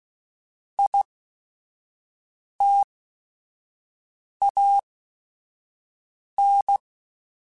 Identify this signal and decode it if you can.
Morse code, heard as a keyed tone, about 16 wpm, 800 Hz. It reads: I T A N
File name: Technical Subjects / Technical Subjects-mp3s / XC @5